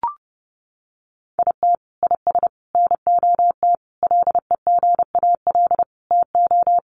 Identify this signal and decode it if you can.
Morse code, heard as a keyed tone, about 30 wpm, 700 Hz. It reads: IT IS NOT LEGAL TO
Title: Sentences from Top 1200 Words 30wpm